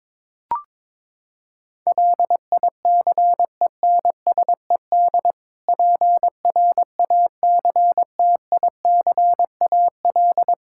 Morse code, heard as a keyed tone, about 22 wpm, 700 Hz. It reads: LICENSED PRACTICAL
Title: Practice QSO - Exchange 3 - More Info - 22wpm